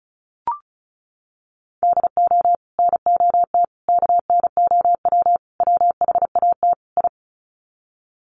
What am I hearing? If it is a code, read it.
Morse code, heard as a keyed tone, about 34 wpm, 700 Hz. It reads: DO NOT KNOW WHAT I
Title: Sets of 5 Words 35wpm